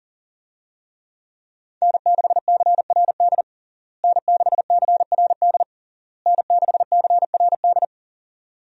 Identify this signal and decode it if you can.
Morse code, heard as a keyed tone, about 40 wpm, 700 Hz. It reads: N6CRD N6CRD N6CRD
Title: USA Call Signs - 5 Characters - Repeated 3x - 40wpm